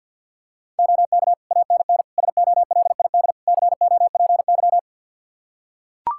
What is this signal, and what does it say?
Morse code, heard as a keyed tone, about 45 wpm, 700 Hz. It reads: OK ANN SOLID COPY
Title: K1USN SST QSO Practice 50wpm